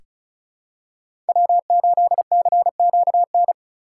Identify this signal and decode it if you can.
Morse code, heard as a keyed tone, about 36 wpm, 700 Hz. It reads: W8CQN
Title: USA Call Signs - 5 Characters - 3x Character Spacing - 35wpm